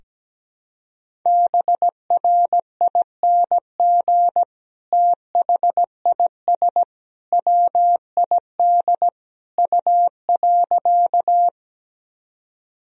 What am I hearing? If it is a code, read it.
Morse code, heard as a keyed tone, about 17 wpm, 700 Hz. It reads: BRING THIS WID U.
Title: Sentences from Top 500 Words and Abbreviations 17wpm